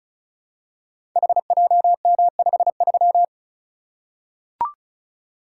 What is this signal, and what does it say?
Morse code, heard as a keyed tone, about 36 wpm, 700 Hz. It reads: HJM53
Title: Code Groups - 5 Letter:Numbers - 35wpm